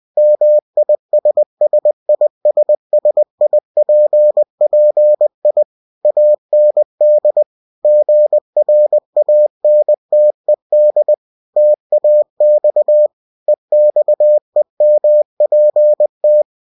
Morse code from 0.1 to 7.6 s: MISSISSIPPI AND